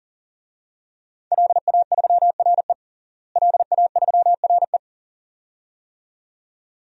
Morse code, heard as a keyed tone, about 40 wpm, 700 Hz. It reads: LA3RE LA3RE
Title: DXCC-100 Call Signs - 5 Characters - Repeated 2x - 40wpm